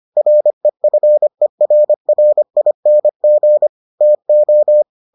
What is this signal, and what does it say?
Morse code, heard as a keyed tone, about 25 wpm, 600 Hz. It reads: REFERRING TO